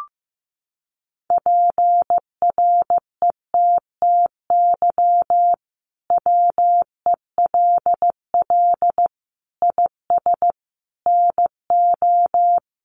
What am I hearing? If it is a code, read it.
Morse code, heard as a keyed tone, about 15 wpm, 700 Hz. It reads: PRETTY WELL IS NO
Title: Sentences from Top 1000 Words 15wpm